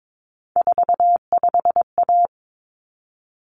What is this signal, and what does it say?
Morse code, heard as a keyed tone, about 22 wpm, 700 Hz. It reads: 45A
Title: Common Cut Numbers - 3-Digit 22wpm